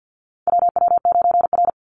Morse code, heard as a keyed tone, about 50 wpm, 700 Hz. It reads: WW9R
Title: Call Signs Repeated 3x - 50wpm